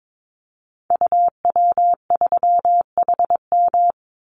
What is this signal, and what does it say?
Morse code, heard as a keyed tone, about 22 wpm, 700 Hz. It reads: UW3HM
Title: Call Signs - 22wpm